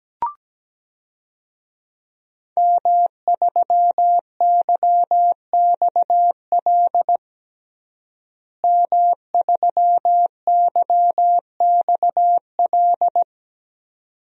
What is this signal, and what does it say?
Morse code, heard as a keyed tone, about 17 wpm, 700 Hz. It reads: M3YXL M3YXL
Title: DXCC-100 Call Signs - 5 Characters - Repeated 2x - 17wpm